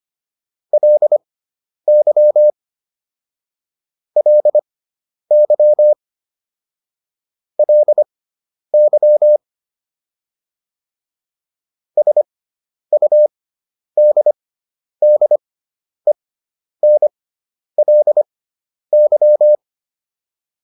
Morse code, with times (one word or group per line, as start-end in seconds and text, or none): 0.7-2.5 LY
4.2-5.9 LY
7.6-9.4 LY
12.0-19.6 SUDDENLY